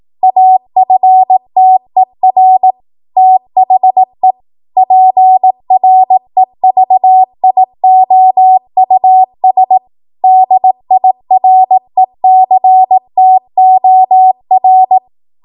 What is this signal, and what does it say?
Morse code, heard as a keyed tone, about 18 wpm, 750 Hz. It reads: AFTER THE PREVIOUS DIRECTOR